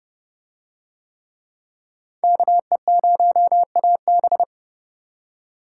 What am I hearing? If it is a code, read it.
Morse code, heard as a keyed tone, about 29 wpm, 700 Hz. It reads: KE0AB